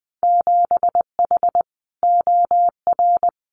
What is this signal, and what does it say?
Morse code, heard as a keyed tone, about 20 wpm, 700 Hz. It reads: 7H OR
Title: Winter Field Day Exchanges Encore 20wpm